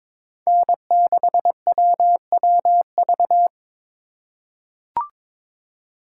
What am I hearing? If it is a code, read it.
Morse code, heard as a keyed tone, about 22 wpm, 700 Hz. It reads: N6WWV
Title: USA Call Signs - 5 Characters - 3x Character Spacing - 22wpm